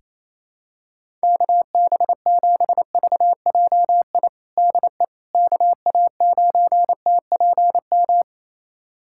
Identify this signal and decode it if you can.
Morse code, heard as a keyed tone, about 28 wpm, 700 Hz. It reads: KB7VJI DE KA9TPM
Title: USA Callsign DE Callsign - 6 Characters - 28wpm